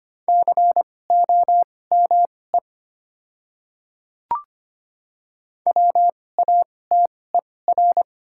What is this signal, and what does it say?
Morse code, heard as a keyed tone, about 25 wpm, spaced as for 18 wpm, 700 Hz. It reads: COME WATER
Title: 2x Character Spacing - Top 100 Words - 25wpm